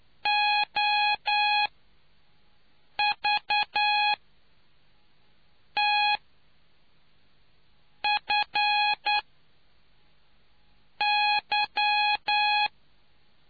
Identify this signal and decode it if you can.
Morse code, heard as a keyed tone, about 9 wpm, spaced as for 4 wpm, 830 Hz. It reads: OVTFY